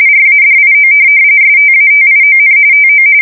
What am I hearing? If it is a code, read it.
RTTY, about 45.45 baud. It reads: RADIO STATION TO
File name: fsk170 / teletype-RTTY